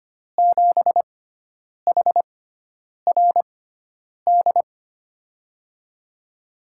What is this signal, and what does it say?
Morse code, heard as a keyed tone, about 25 wpm, spaced as for 9 wpm, 700 Hz. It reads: 7HRD
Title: USA Call Signs - 6x Character Spacing - 25wpm